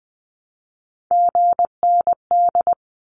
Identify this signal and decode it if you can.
Morse code, heard as a keyed tone, about 20 wpm, 700 Hz. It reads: GND
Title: Abbreviations Repeated 3x - 20wpm